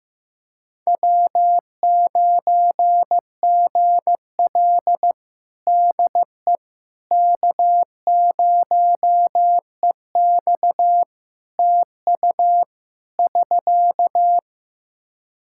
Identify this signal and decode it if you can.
Morse code, heard as a keyed tone, about 14 wpm, 700 Hz. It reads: W9GL DE K0EX TU <SK>
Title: Practice QSO - Exchange 4 - Ending - 15wpm